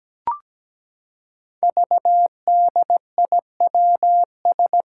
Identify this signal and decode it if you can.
Morse code, heard as a keyed tone, about 17 wpm, 700 Hz. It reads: VDIWS
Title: Five Challenging Letter-Number Combinations 17wpm